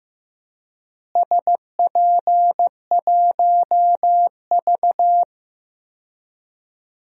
Morse code, heard as a keyed tone, about 15 wpm, 700 Hz. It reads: SP1V
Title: DXCC-100 Call Signs - 4 Characters - Repeated 3x - 15wpm